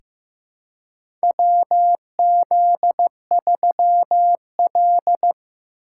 Morse code, heard as a keyed tone, about 15 wpm, 700 Hz. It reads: WZ3L
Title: USA Call Signs - 4 Characters - 3x Character Spacing - 15wpm